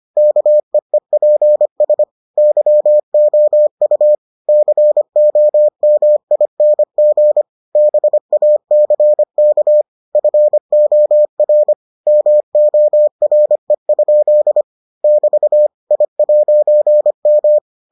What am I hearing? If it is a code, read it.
Morse code, heard as a keyed tone, about 25 wpm, 600 Hz. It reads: KEEPS YOU COMING BACK FOR MORE? <BT> I'M